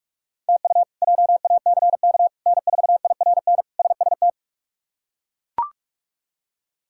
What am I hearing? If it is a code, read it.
Morse code, heard as a keyed tone, about 42 wpm, 700 Hz. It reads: TU JACK N4IRN SST
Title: K1USN SST QSO Practice - 45wpm